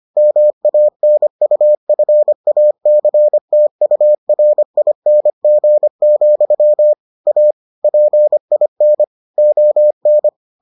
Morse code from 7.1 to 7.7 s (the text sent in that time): A